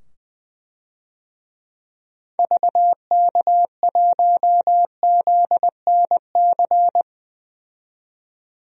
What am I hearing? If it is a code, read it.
Morse code, heard as a keyed tone, about 20 wpm, 700 Hz. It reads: VK1ZNC
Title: DXCC-100 Call Signs - 6 Characters - 20wpm